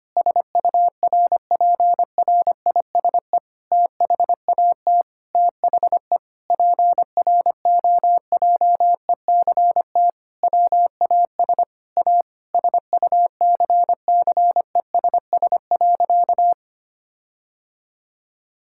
Morse code, heard as a keyed tone, about 25 wpm, 700 Hz. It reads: SURPRISE THAT THE PROJECT WAS A SUCCESS.
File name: practice / 25wpm / Sentences with 9-Word Phrases - 25wpm